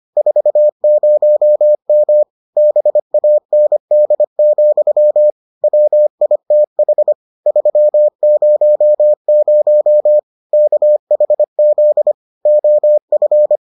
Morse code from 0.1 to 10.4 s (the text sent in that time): 40M BAND, WITH 300